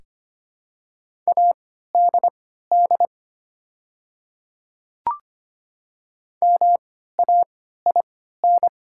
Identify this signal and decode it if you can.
Morse code, heard as a keyed tone, about 25 wpm, spaced as for 14 wpm, 700 Hz. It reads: ADD MAIN
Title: 3x Character Spacing - Top 300 Words - 25wpm